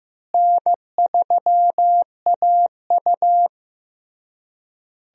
Morse code, heard as a keyed tone, about 16 wpm, 700 Hz. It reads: N3AU
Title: USA Call Signs - 4 Characters - Repeated 3x - 15wpm